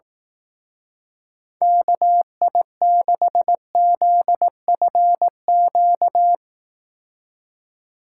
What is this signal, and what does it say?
Morse code, heard as a keyed tone, about 18 wpm, 700 Hz. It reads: KI6ZFQ
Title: Speed Racing - USA 2x3 Call Signs 15wpm